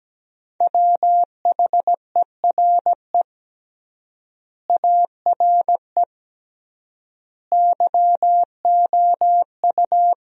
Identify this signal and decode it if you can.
Morse code, heard as a keyed tone, about 17 wpm, 700 Hz. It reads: WHERE ARE YOU
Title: 3x Word-Spacing - Sentences from Top 200 Words - 17wpm